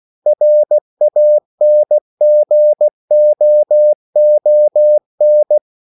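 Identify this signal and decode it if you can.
Morse code, heard as a keyed tone, about 16 wpm, 600 Hz. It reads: RANGOON
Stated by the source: CW practice of national capitals at 16 words per minute, from ON6ZQ.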